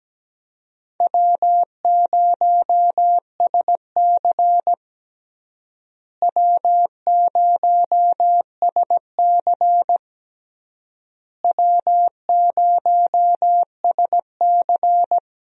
Morse code, heard as a keyed tone, about 17 wpm, 700 Hz. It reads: W0SC W0SC W0SC
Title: USA Call Signs - 4 Characters - Repeated 3x - 17wpm